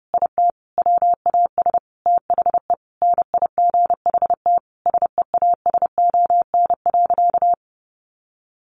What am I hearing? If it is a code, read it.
Morse code, heard as a keyed tone, about 30 wpm, 700 Hz. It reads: IT WAS THE NIGHT SEASON.
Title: Sentences from Top 1400 Words 30wpm